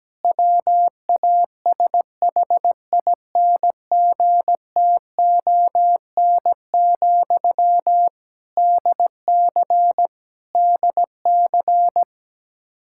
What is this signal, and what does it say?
Morse code, heard as a keyed tone, about 17 wpm, 700 Hz. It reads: WASHINGTON, DC DC
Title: Practice QSO - Exchange 1 - Beginning - 17wpm